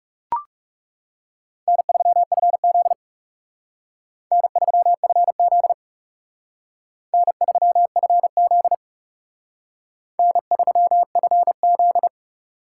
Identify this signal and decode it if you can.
Morse code, heard as a keyed tone, about 37 wpm, 700 Hz. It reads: N3FZ N3FZ N3FZ N3FZ
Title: Speed Racing - USA 1x2 Call Signs 30wpm